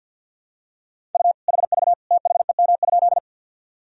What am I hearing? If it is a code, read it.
Morse code, heard as a keyed tone, about 50 wpm, 700 Hz. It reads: U HV THEM?
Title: Sentences from Top 100 Words and Abbreviations 50wpm